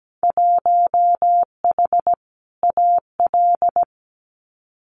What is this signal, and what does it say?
Morse code, heard as a keyed tone, about 17 wpm, 700 Hz. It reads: 1H AL
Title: Winter Field Day Exchanges 17wpm